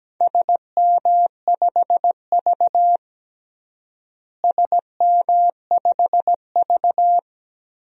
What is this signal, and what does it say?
Morse code, heard as a keyed tone, about 17 wpm, 700 Hz. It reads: SM5V SM5V
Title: DXCC-100 Call Signs - 4 Characters - Repeated 2x - 17wpm